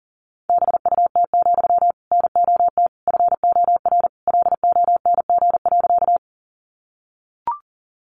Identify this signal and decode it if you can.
Morse code, heard as a keyed tone, about 40 wpm, 700 Hz. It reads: BUT, NOT FOR LONG.
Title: Sentences from Top 600 Words 40wpm